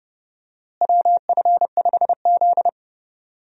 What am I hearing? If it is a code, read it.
Morse code, heard as a keyed tone, about 30 wpm, 700 Hz. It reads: WF5Z
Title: USA Call Signs - 4 Characters - Repeated 3x - 30wpm